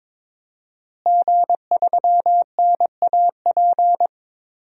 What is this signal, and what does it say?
Morse code, heard as a keyed tone, about 22 wpm, 700 Hz. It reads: G3NAP
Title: DXCC-100 Call Signs - 5 Characters - Repeated 3x - 22wpm